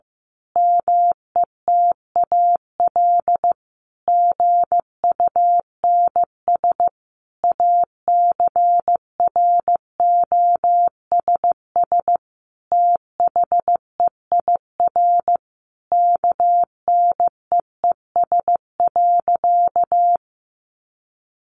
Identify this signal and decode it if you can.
Morse code, heard as a keyed tone, about 15 wpm, 700 Hz. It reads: METAL GUNS ACROSS THEIR KNEES.
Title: A Sound Of Thunder 15wpm